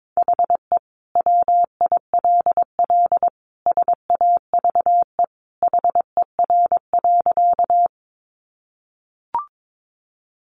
Morse code, heard as a keyed tone, about 22 wpm, 700 Hz. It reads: HE WILL SAVE HER.
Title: Sentences from Top 1100 Words 22wpm